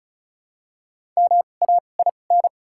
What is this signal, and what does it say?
Morse code, heard as a keyed tone, about 34 wpm, spaced as for 25 wpm, 700 Hz. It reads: MAIN
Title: 2x Character Spacing - Top 500 Words - 35wpm